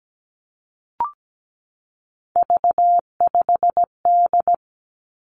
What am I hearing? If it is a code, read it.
Morse code, heard as a keyed tone, about 17 wpm, 700 Hz. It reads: V5D
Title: Three Challenging Letter-Number Combinations 17wpm